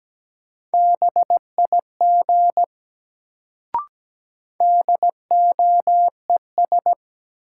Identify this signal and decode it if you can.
Morse code, heard as a keyed tone, about 17 wpm, 700 Hz. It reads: BIG DOES
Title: Sentences from Top 200 Words - New Word Drill - 17wpm